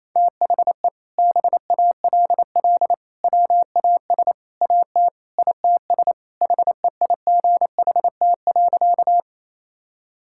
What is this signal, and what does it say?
Morse code, heard as a keyed tone, about 28 wpm, 700 Hz. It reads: THE BALL WAS AT ITS HEIGHT.
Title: Sentences from Top 1600 Words 28wpm